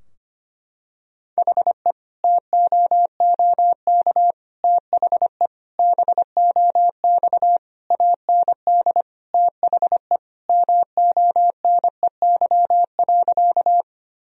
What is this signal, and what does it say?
Morse code, heard as a keyed tone, about 25 wpm, 700 Hz. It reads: HE TOOK THE BOX AND THE MONEY.
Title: Sentences from Top 1000 Words - Review Part 2 - 25wpm